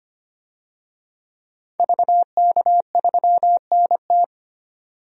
Morse code, heard as a keyed tone, about 25 wpm, 700 Hz. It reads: VK3NT
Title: DXCC-100 Call Signs - 5 Characters - 25wpm